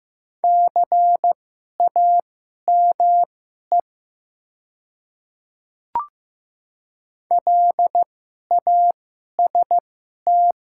Morse code, heard as a keyed tone, about 15 wpm, spaced as for 11 wpm, 690 Hz. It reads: CAME LAST
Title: 2x Character Spacing - Top 300 Words - 15wpm